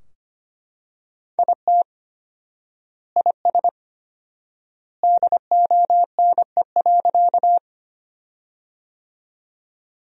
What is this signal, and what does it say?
Morse code, heard as a keyed tone, about 25 wpm, 700 Hz. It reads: IT IS DONE.
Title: 4x Word-Spacing - Sentences from Top 500 Words - 25wpm